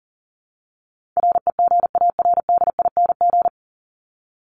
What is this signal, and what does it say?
Morse code, heard as a keyed tone, about 40 wpm, 700 Hz. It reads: REGARDING
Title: ING Suffix - Top 100 Words with Base Words 40wpm